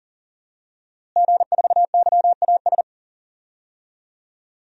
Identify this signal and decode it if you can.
Morse code, heard as a keyed tone, about 40 wpm, 700 Hz. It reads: G4YAS
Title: DXCC-100 Call Signs - 5 Characters - Repeated 2x - 40wpm